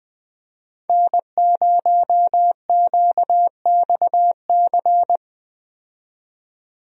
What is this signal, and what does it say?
Morse code, heard as a keyed tone, about 20 wpm, 700 Hz. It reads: N0QXC
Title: USA Call Signs - 5 Characters - 20wpm